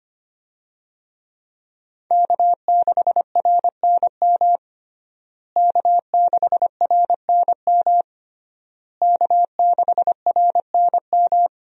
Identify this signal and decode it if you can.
Morse code, heard as a keyed tone, about 25 wpm, 700 Hz. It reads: K6RNM K6RNM K6RNM
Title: USA Call Signs - 5 Characters - Repeated 3x - 25wpm